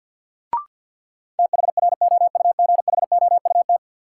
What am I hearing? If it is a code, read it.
Morse code, heard as a keyed tone, about 51 wpm, 700 Hz. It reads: THROUGHOUT
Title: Sentences from Top 800 Words - New Word Drill - 50wpm